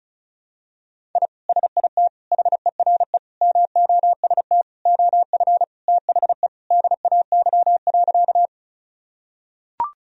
Morse code, heard as a keyed tone, about 36 wpm, 700 Hz. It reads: I SIT HERE MOST OF THE DAY.
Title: Sentences from Top 1000 Words - Review Part 1 - 35wpm